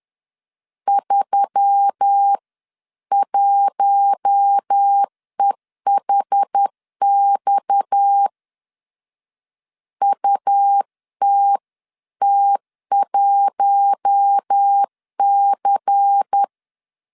Morse code, heard as a keyed tone, about 10 wpm, 800 Hz. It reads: 31EHX UTT1C
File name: CW / DET / 1 9 1500C S A 10wpm 4511